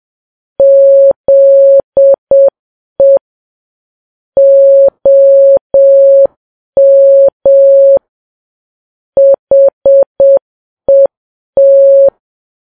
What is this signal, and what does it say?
Morse code, heard as a keyed tone, about 7 wpm, 550 Hz. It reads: ZE OM HET
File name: Files Web tls / Morse on9fv / morse7kps